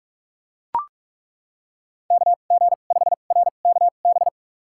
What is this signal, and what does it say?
Morse code, heard as a keyed tone, about 41 wpm, spaced as for 31 wpm, 700 Hz. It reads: KG5RKB
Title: USA Call Signs - 6 Characters - 2x Character Spacing - 45wpm